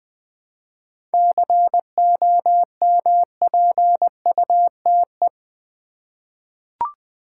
Morse code, heard as a keyed tone, about 20 wpm, 700 Hz. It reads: COMPUTE
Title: 7 Letter Words - First 3 Letters Given - 20wpm